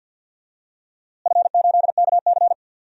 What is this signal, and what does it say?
Morse code, heard as a keyed tone, about 52 wpm, 700 Hz. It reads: F8KC
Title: DXCC-100 Call Signs - 4 Characters - Repeated 3x - 50wpm